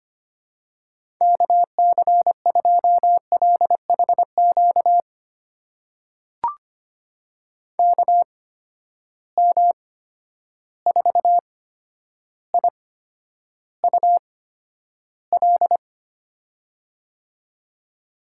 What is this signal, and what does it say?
Morse code, heard as a keyed tone, about 25 wpm, spaced as for 11 wpm, 700 Hz. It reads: KC2LHQ KM4IUL
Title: USA Call Signs - 6 Characters - 8x Character Spacing - 25wpm